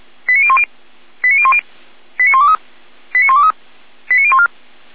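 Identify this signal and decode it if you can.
Five-tone selective call (ZVEI), heard as quick runs of five tones, tones of about 70 ms. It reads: 89010 89010 89123 89123 89014
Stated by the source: SDR-received ZVEI five-tone selective call, address 89022.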